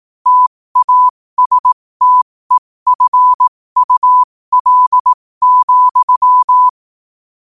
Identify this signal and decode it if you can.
Morse code, heard as a keyed tone, about 18 wpm, spaced as for 15 wpm, 1000 Hz. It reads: TASTEFUL,
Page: New quotes every day in morse code at 15 Words per minute.